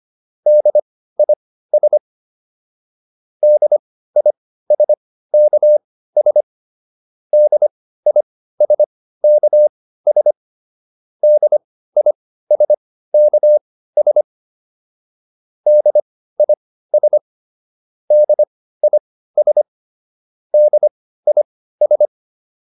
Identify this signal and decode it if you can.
Morse code, heard as a keyed tone, about 25 wpm, spaced as for 15 wpm, 600 Hz. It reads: DIS DISKS DISKS DISKS DIS DIS DIS